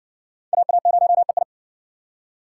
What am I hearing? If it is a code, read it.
Morse code, heard as a keyed tone, about 55 wpm, 700 Hz. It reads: AA0EI